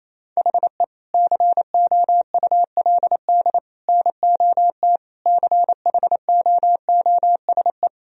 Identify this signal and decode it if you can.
Morse code, heard as a keyed tone, about 28 wpm, 700 Hz. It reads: HE COULD NOT CHOOSE
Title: Sentences from Top 1000 Words - Review Part 4 - 28wpm